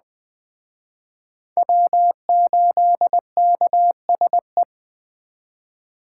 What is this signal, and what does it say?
Morse code, heard as a keyed tone, about 20 wpm, 700 Hz. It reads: W8KSE
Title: Speed Racing - USA 1x3 Call Signs 17wpm